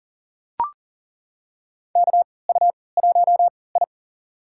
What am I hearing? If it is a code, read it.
Morse code, heard as a keyed tone, about 40 wpm, spaced as for 23 wpm, 700 Hz. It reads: KU1I
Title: USA Call Signs - 4 Characters - 3x Character Spacing - 40wpm